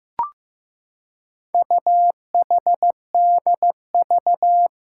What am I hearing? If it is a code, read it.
Morse code, heard as a keyed tone, about 15 wpm, 700 Hz. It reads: UHDV
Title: Four Challenging Letter-Number Combinations 15wpm